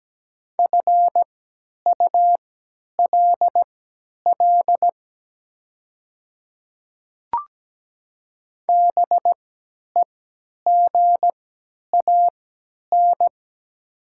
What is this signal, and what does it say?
Morse code, heard as a keyed tone, about 17 wpm, spaced as for 10 wpm, 700 Hz. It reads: FULL BEGAN
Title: 3x Character Spacing - Top 500 Words - 17wpm